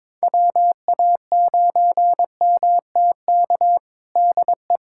Morse code, heard as a keyed tone, about 22 wpm, 700 Hz. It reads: WA9MTK DE
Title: USA Callsign DE Callsign - 6 Characters - 22wpm